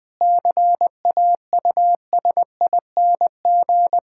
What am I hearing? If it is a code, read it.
Morse code, heard as a keyed tone, about 20 wpm, 700 Hz. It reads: CAUSING
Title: Sentences with 9-Word Phrases - 20wpm